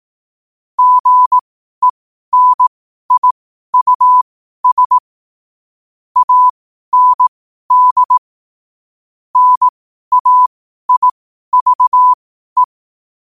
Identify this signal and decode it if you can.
Morse code, heard as a keyed tone, about 18 wpm, spaced as for 12 wpm, 1000 Hz. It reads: GENIUS AND NAIVE
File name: qotdfiles / 12wpm